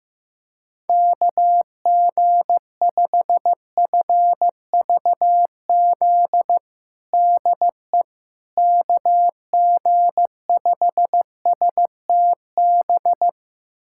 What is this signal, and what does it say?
Morse code, heard as a keyed tone, about 15 wpm, 700 Hz. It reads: KG5FVZ DE KG5STB
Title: USA Callsign DE Callsign - 6 Characters - 15wpm